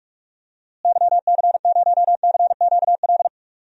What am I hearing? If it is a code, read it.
Morse code, heard as a keyed tone, about 41 wpm, 700 Hz. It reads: YC0CQL
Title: DXCC-100 Call Signs - 6 Characters - Repeated 3x - 45wpm